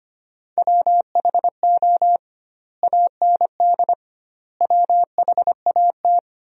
Morse code, heard as a keyed tone, about 25 wpm, 700 Hz. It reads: WHO AND WHAT
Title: 2x Word-Spacing - Sentences from Top 100 Words - 25wpm